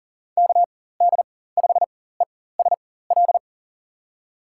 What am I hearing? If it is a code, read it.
Morse code, heard as a keyed tone, about 40 wpm, spaced as for 19 wpm, 700 Hz. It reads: KD5ESL
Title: USA Call Signs - 6 Characters - 4x Character Spacing - 40wpm